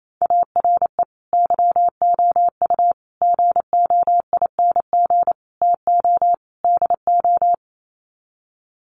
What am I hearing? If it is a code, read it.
Morse code, heard as a keyed tone, about 28 wpm, 700 Hz. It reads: ARE YOU GOING TO DO
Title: Sets of 5 Words 28wpm